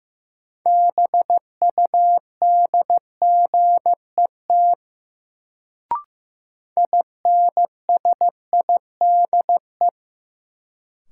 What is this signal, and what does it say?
Morse code, heard as a keyed tone, about 15 wpm, 700 Hz. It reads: BUDGET INSIDE
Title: Sentences from Top 1200 Words - New Word Drill Alt - 15wpm